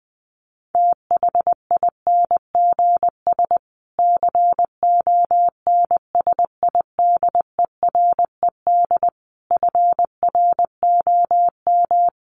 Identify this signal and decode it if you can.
Morse code, heard as a keyed tone, about 20 wpm, 700 Hz. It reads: THINGS CONSIDERED FROM
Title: Sets of 5 Words Encore 20wpm